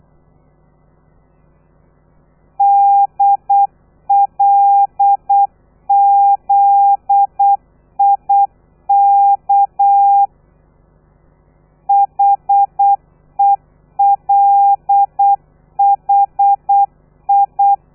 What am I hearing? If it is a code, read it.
Morse code, heard as a keyed tone, about 8 wpm, 800 Hz. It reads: DLZIK HELHI
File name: telegrafia / i2tfj / Lez13 35